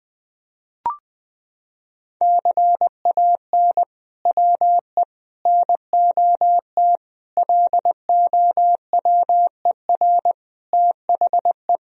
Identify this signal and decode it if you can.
Morse code, heard as a keyed tone, about 20 wpm, 700 Hz. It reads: CAN WE NOT LOWER THE
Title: Sentences from Top 1000 Words - Review Part 4 - 20wpm